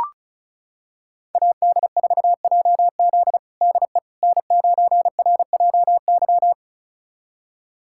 Morse code, heard as a keyed tone, about 34 wpm, 700 Hz. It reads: AD4JZ DE N9RJY
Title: USA Callsign DE Callsign - 5 Characters - 35wpm